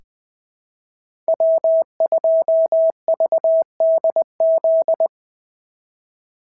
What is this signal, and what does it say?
Morse code, heard as a keyed tone, about 20 wpm, 650 Hz. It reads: W2VDZ
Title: POTA QSO Practice - 20wpm